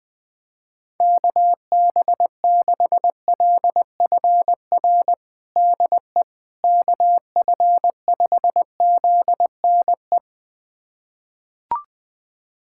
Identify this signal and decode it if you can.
Morse code, heard as a keyed tone, about 21 wpm, 700 Hz. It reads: KB6LFR DE KF5ZNE
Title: USA Callsign DE Callsign 20wpm